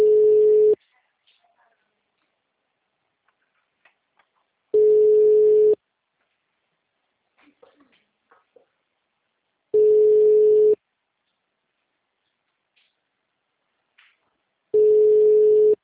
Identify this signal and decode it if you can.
Ringback tone (Europe, 425 Hz), with speech.